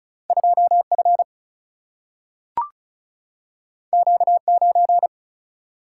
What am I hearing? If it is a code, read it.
Morse code, heard as a keyed tone, about 36 wpm, 700 Hz. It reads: 2 F Q 9
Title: Single-Letter Single-Number 35wpm